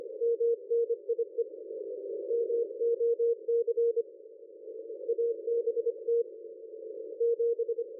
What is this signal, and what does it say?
Morse code, heard as a keyed tone, about 25 wpm, 470 Hz. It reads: MNIE MOC ABT 7
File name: dlazaawansowanych / nagraniazpasma / ExpertSDR2 line out 23.01.2020 14.04.01